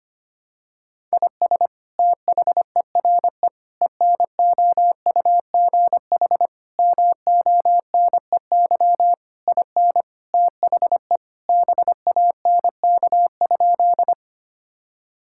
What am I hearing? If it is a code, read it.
Morse code, heard as a keyed tone, about 25 wpm, 700 Hz. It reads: IS THERE ENOUGH MONEY IN THE BANK?
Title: Sentences from Top 1000 Words - Review Part 6 - 25wpm